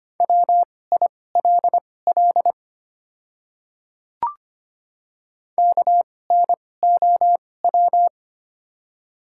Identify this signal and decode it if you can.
Morse code, heard as a keyed tone, about 25 wpm, spaced as for 18 wpm, 700 Hz. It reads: WILL KNOW
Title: 2x Character Spacing - Top 100 Words - 25wpm